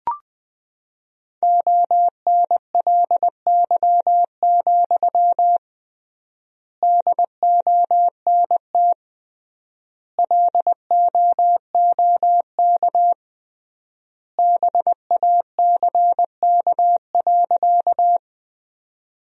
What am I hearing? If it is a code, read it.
Morse code, heard as a keyed tone, about 21 wpm, 700 Hz. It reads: ONLY, DONT LOOK BACK.
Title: 3x Word-Spacing - Sentences from Top 300 Words - 20wpm